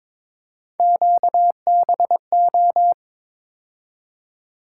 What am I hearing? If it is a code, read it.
Morse code, heard as a keyed tone, about 22 wpm, 700 Hz. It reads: QBO
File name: practice / 22wpm / Three Letter-Number Combinations - 22wpm